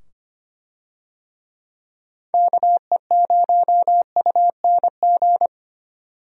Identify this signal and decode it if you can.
Morse code, heard as a keyed tone, about 25 wpm, 700 Hz. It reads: KE0UNG
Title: USA Call Signs - 6 Characters - 25wpm